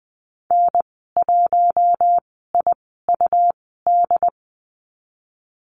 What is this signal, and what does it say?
Morse code, heard as a keyed tone, about 20 wpm, spaced as for 14 wpm, 700 Hz. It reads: N1IUD
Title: 140 Lesson 23g - Letter f - Call Signs 20wpm